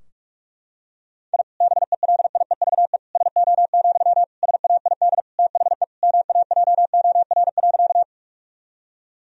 Morse code, heard as a keyed tone, about 41 wpm, 700 Hz. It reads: I BELIEVE SO, SAID THE MAJOR.
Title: Sentences from Top 1000 Words - Review Part 4 - 45wpm